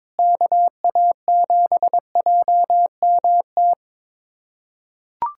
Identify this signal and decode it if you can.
Morse code, heard as a keyed tone, about 22 wpm, 700 Hz. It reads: KA7JMT
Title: USA Callsign DE Callsign 22wpm